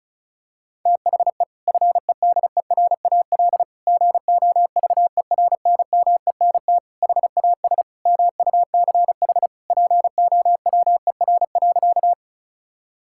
Morse code, heard as a keyed tone, about 36 wpm, 690 Hz. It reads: THE FEDERAL GOVERNMENT HAS MUCH POWER.
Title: Sentences from Top 1000 Words - Review Part 5 - 35wpm